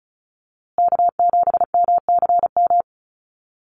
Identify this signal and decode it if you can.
Morse code, heard as a keyed tone, about 34 wpm, 700 Hz. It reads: K7MCM
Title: USA Call Signs Repeated 3x 35wpm